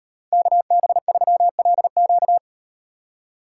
Morse code, heard as a keyed tone, about 36 wpm, 700 Hz. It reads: KB3LQ